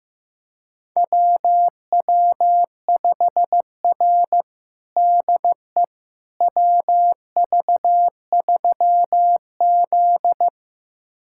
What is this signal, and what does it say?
Morse code, heard as a keyed tone, about 15 wpm, 700 Hz. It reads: WW5R DE WV3Z
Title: USA Callsign DE Callsign - 4 Characters - 15wpm